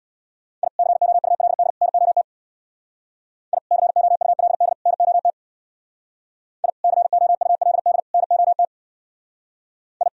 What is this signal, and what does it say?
Morse code, heard as a keyed tone, about 62 wpm, 700 Hz. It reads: I COULD NOT I COULD NOT I COULD NOT I
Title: Speed-Racing - Sets of 3 Words 50wpm